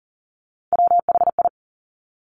Morse code, heard as a keyed tone, about 41 wpm, 700 Hz. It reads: WHI
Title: Top Three Letter Combinations in Words 40wpm